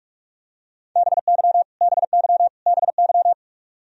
Morse code, heard as a keyed tone, about 46 wpm, 700 Hz. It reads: BY BY BY
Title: Top 100 Words Repeated 45wpm